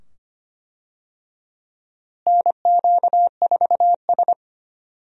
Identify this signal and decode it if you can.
Morse code, heard as a keyed tone, about 25 wpm, 700 Hz. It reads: NQ4S
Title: USA Call Signs - 4 Characters - Repeated 2x - 25wpm